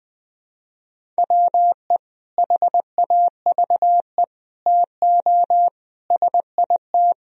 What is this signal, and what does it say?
Morse code, heard as a keyed tone, about 19 wpm, 700 Hz. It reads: WE HAVE TO SIT
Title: Sentences from Top 1000 Words - Review Part 1 - 20wpm